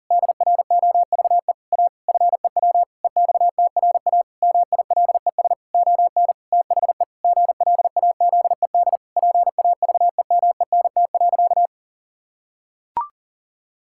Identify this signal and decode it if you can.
Morse code, heard as a keyed tone, about 40 wpm, 700 Hz. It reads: DROVE A FEW EXTRA MILES ON THE GLAZED PAVEMENT.
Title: Pangram Challenge v2 - 40wpm